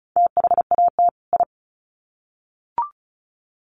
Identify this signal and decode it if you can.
Morse code, heard as a keyed tone, about 34 wpm, 700 Hz. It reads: THAT I
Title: Sets of 2 Words 35wpm